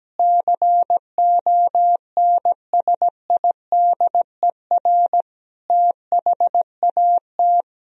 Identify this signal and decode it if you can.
Morse code, heard as a keyed tone, about 17 wpm, 700 Hz. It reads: CONSIDER THAT
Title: Sentences from Top 1000 Words - Review Part 4 - 17wpm